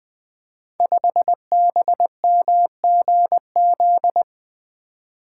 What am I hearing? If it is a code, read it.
Morse code, heard as a keyed tone, about 20 wpm, 700 Hz. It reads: 5BMGZ
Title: Code Groups - 5 Letter:Numbers - 20wpm